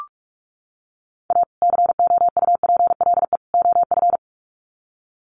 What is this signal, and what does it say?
Morse code, heard as a keyed tone, about 44 wpm, 700 Hz. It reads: A COUPLE OF
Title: Sets of 3 Words 45wpm